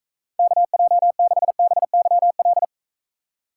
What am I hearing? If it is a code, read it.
Morse code, heard as a keyed tone, about 40 wpm, 700 Hz. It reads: KJ6BYL